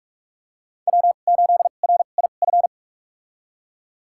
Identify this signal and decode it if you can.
Morse code, heard as a keyed tone, about 44 wpm, spaced as for 32 wpm, 700 Hz. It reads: W8RIF
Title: USA Call Signs - 5 Characters - 2x Character Spacing - 45wpm